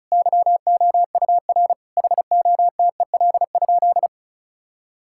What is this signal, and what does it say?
Morse code, heard as a keyed tone, about 34 wpm, 700 Hz. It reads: YOUR HOTEL?
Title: Sentences from Top 1600 Words 35wpm